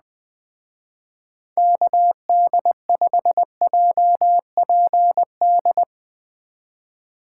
Morse code, heard as a keyed tone, about 20 wpm, 700 Hz. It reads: KD5JPD
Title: Speed Racing - USA 2x3 Call Signs 17wpm